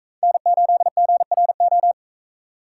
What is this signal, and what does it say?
Morse code, heard as a keyed tone, about 41 wpm, 700 Hz. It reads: N8GRO